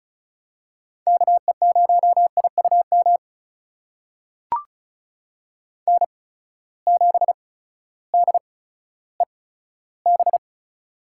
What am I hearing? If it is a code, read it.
Morse code, heard as a keyed tone, about 34 wpm, spaced as for 16 wpm, 700 Hz. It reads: KE0IUM N7DEB
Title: USA Call Signs - 8x Character Spacing - 35wpm